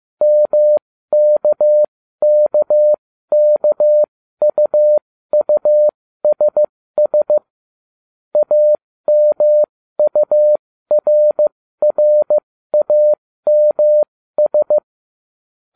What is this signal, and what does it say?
Morse code, heard as a keyed tone, about 15 wpm, spaced as for 13 wpm, 600 Hz. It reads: MKKKUUSS AMURRAMS